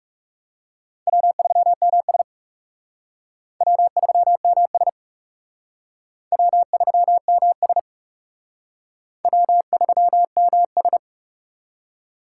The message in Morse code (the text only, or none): W3MS W3MS W3MS W3MS